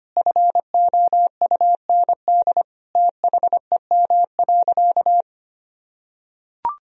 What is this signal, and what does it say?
Morse code, heard as a keyed tone, about 25 wpm, 700 Hz. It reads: FOUND THEM.
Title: Sentences from Top 1000 Words - Review Part 1 - 25wpm